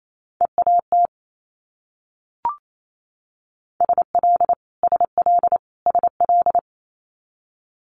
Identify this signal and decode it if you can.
Morse code, heard as a keyed tone, about 28 wpm, 700 Hz. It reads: EAT SL SL SL
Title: Multi-letter Phonemes Repeated 3x 28wpm